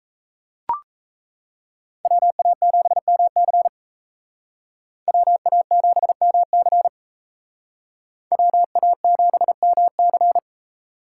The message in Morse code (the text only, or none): WA7MC WA7MC WA7MC